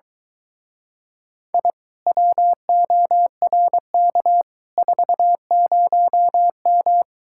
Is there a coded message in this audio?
Yes: Morse code from 1.5 to 7.0 s, heard as a keyed tone, about 23 wpm, 700 Hz. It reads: I WORK 40M